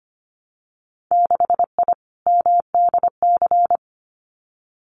Morse code, heard as a keyed tone, about 25 wpm, 700 Hz. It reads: 6I MDC
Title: Winter Field Day Exchanges Encore 25wpm